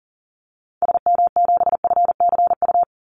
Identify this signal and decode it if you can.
Morse code, heard as a keyed tone, about 40 wpm, 700 Hz. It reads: SM7FCU
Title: Call Signs Repeated 2x - 40wpm